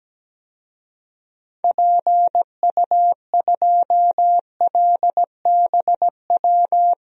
Morse code, heard as a keyed tone, about 17 wpm, 700 Hz. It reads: PU2LBW
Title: DXCC-100 Call Signs - 6 Characters - Repeated 3x - 17wpm